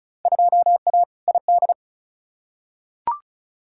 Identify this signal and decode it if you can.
Morse code, heard as a keyed tone, about 36 wpm, 700 Hz. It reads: 2A ID
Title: Field Day Exchanges - 35wpm